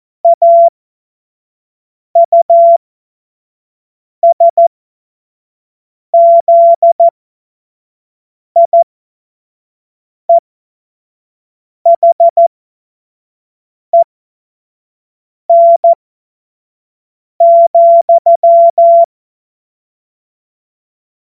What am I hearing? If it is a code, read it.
Morse code, heard as a keyed tone, about 14 wpm, spaced as for 5 wpm, 680 Hz. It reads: AUSZIEHEN,
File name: Morsekurs / Witze / Witz 19 März 2024 25 F70